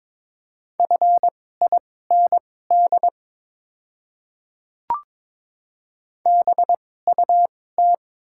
Morse code, heard as a keyed tone, about 22 wpm, spaced as for 16 wpm, 690 Hz. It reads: FIND BUT
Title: 2x Character Spacing - Top 100 Words - 22wpm